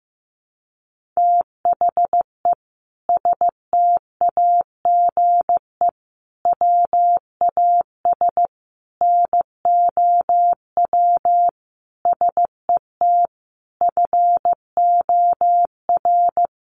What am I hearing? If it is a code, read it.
Morse code, heard as a keyed tone, about 15 wpm, 700 Hz. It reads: THE STAGE WAS NOW SET FOR
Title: Sentences from Top 1500 Words 15wpm